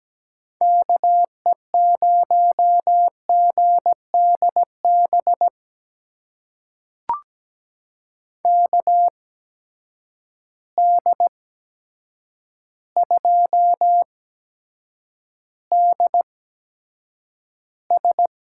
Morse code, heard as a keyed tone, about 17 wpm, spaced as for 8 wpm, 700 Hz. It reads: KE0GDB KD2DS
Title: USA Call Signs - 6 Characters - 8x Character Spacing - 17wpm